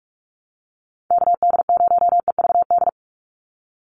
Morse code, heard as a keyed tone, about 44 wpm, 700 Hz. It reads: KD0EVD
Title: USA Call Signs 45wpm